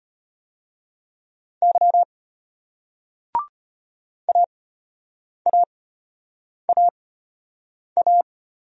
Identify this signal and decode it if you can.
Morse code, heard as a keyed tone, about 34 wpm, spaced as for 8 wpm, 700 Hz. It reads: Y AAAA